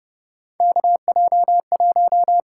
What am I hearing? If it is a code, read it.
Morse code, heard as a keyed tone, about 30 wpm, 700 Hz. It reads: KJ1
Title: USA Call Signs - 4 Characters - 6x Character Spacing - 30wpm